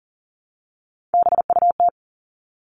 Morse code, heard as a keyed tone, about 38 wpm, 700 Hz. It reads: BUT
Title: Top 500 Words 40wpm